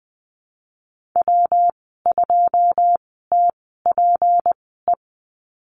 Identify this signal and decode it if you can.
Morse code, heard as a keyed tone, about 20 wpm, spaced as for 15 wpm, 700 Hz. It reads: W2TPE
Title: 245 Lesson 39g - Letter z - Call Signs 20wpm